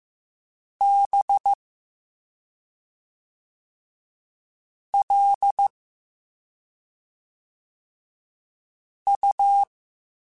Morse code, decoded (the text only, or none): B L U